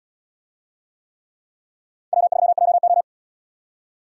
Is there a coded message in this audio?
Yes: Morse code from 2.1 to 3.0 s, heard as a keyed tone, about 77 wpm, 700 Hz. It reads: W3JO